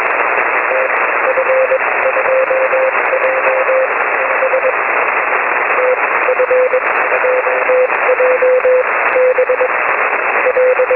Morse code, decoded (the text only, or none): none